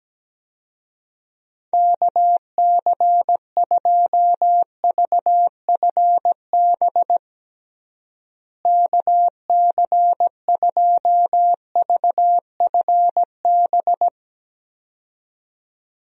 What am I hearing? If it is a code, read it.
Morse code, heard as a keyed tone, about 17 wpm, 700 Hz. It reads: KC2VFB KC2VFB
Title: USA Call Signs - 6 Characters - Repeated 2x - 17wpm